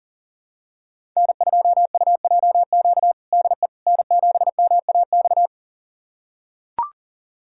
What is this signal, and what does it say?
Morse code, heard as a keyed tone, about 40 wpm, 700 Hz. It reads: N2UJQ DE N7MAX
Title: USA Callsign DE Callsign - 5 Characters - 40wpm